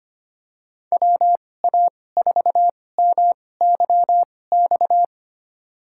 Morse code, heard as a keyed tone, about 25 wpm, spaced as for 18 wpm, 700 Hz. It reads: WA4MYX
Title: USA Call Signs - 2x Character Spacing - 25wpm